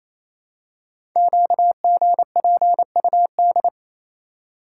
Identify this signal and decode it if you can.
Morse code, heard as a keyed tone, about 29 wpm, 700 Hz. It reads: QGPUD
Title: Code Groups - 5 Letters - 28wpm